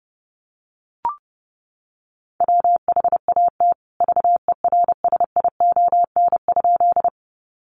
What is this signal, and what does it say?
Morse code, heard as a keyed tone, about 28 wpm, 700 Hz. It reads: WHAT VERSION?
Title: Sentences from Top 1400 Words 30wpm